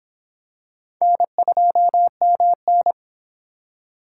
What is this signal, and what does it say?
Morse code, heard as a keyed tone, about 26 wpm, 700 Hz. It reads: N2MN